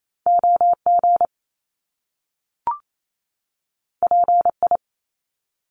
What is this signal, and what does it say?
Morse code, heard as a keyed tone, about 28 wpm, 700 Hz. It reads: OG PI
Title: Top Two Letter Combinations in Words 28wpm